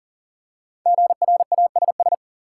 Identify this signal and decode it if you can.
Morse code, heard as a keyed tone, about 40 wpm, 700 Hz. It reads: GRASS
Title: Five Letter Words - 40wpm